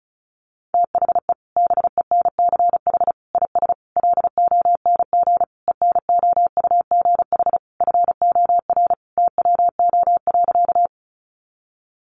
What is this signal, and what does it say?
Morse code, heard as a keyed tone, about 34 wpm, 700 Hz. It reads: THE BENCH IS LONG ENOUGH FOR TWO.
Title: Sentences from Top 1400 Words 35wpm